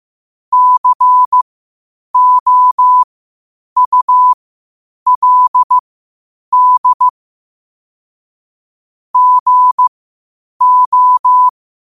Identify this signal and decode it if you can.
Morse code, heard as a keyed tone, about 15 wpm, spaced as for 8 wpm, 1000 Hz. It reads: COULD GO
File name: qotdfiles / 8wpm